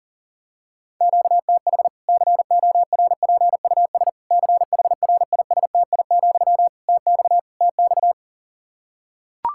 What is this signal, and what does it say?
Morse code, heard as a keyed tone, about 40 wpm, 700 Hz. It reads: QTH CORPUS CHRISTI, TX TX
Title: Practice QSO - Exchange 1 - Beginning - 40wpm